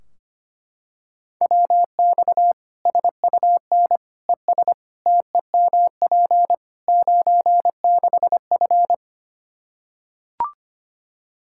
Morse code, heard as a keyed tone, about 25 wpm, 700 Hz. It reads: WX SUN ES TEMP 96F
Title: Practice QSO - Exchange 2 - Info - 25wpm